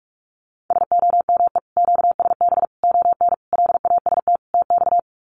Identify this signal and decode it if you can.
Morse code, heard as a keyed tone, about 44 wpm, 700 Hz. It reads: SOME QSB ON LAST TX
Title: Common QSO Phrases 45wpm